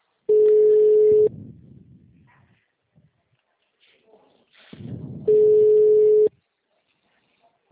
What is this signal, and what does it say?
Ringback tone (Europe, 425 Hz), with speech.